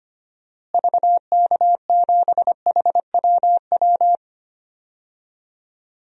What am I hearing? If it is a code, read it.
Morse code, heard as a keyed tone, about 25 wpm, 700 Hz. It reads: VK7HWW
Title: DXCC-100 Call Signs - 6 Characters - Repeated 3x - 25wpm